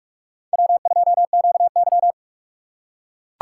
Morse code, heard as a keyed tone, about 41 wpm, 700 Hz. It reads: W2QY